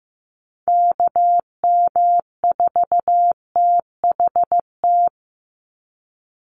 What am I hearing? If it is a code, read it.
Morse code, heard as a keyed tone, about 15 wpm, 700 Hz. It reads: KM4THT
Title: USA Call Signs 15wpm